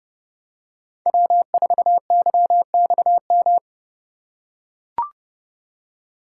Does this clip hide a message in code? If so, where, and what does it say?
Yes: Morse code from 1.1 to 3.6 s, heard as a keyed tone, about 28 wpm, 700 Hz. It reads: W4YXM